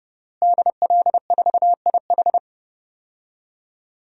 Morse code, heard as a keyed tone, about 30 wpm, 700 Hz. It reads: DL4IH
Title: DXCC-100 Call Signs - 5 Characters - Repeated 3x - 30wpm